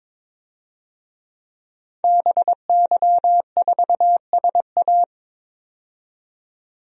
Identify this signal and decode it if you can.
Morse code, heard as a keyed tone, about 22 wpm, 700 Hz. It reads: BY4SA
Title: DXCC-100 Call Signs - 5 Characters - 22wpm